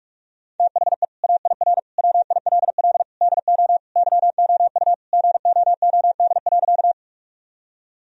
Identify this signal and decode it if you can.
Morse code, heard as a keyed tone, about 46 wpm, 700 Hz. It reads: THE AIR WILL DO YOU GOOD.
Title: Sentences from Top 1000 Words - Review Part 1 - 45wpm